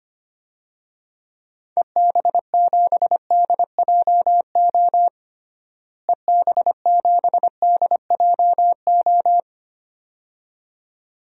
Morse code, heard as a keyed tone, about 25 wpm, 700 Hz. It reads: EB7DJO EB7DJO
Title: DXCC-100 Call Signs - 6 Characters - Repeated 2x - 25wpm